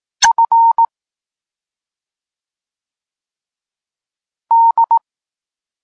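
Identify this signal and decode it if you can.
Morse code, heard as a keyed tone, about 18 wpm, 920 Hz. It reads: F D
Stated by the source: Polish Morse lesson teaching the letter C, from SP9MOA Niepołomicki Klub Krótkofalowców.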